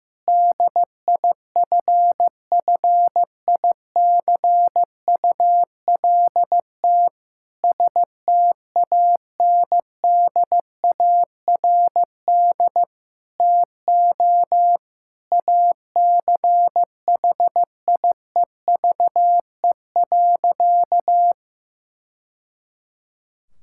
Morse code, heard as a keyed tone, about 16 wpm, 700 Hz. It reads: DIFFICULT STANDARD TO ACHIEVE.
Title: Sentences from Top 1000 Words - Review Part 6 - 15wpm